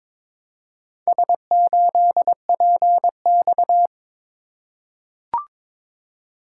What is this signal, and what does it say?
Morse code, heard as a keyed tone, about 22 wpm, 700 Hz. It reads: S8PX
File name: practice / 22wpm / Four Letter-Number Combinations - 22wpm